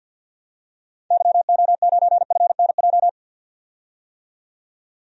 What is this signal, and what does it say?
Morse code, heard as a keyed tone, about 50 wpm, 700 Hz. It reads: YO9FNJ
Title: DXCC-100 Call Signs - 6 Characters - 50wpm